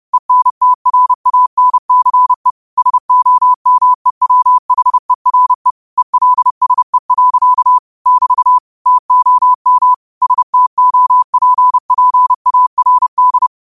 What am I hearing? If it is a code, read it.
Morse code, heard as a keyed tone, about 31 wpm, 1000 Hz. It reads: ENTRANCE SOMEWHERE ELSE. <BT> TOM STOPPARD
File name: qotdfiles / 30wpm